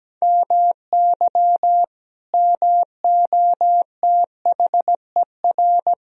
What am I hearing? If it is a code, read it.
Morse code, heard as a keyed tone, about 17 wpm, 700 Hz. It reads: MY MOTHER
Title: Sentences from Top 1000 Words - Review Part 4 - 17wpm